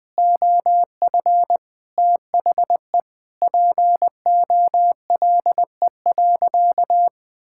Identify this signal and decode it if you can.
Morse code, heard as a keyed tone, about 21 wpm, 700 Hz. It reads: OF THE POLE.
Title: Sentences with 4-Word Phrases - Spoken Beginning - 20wpm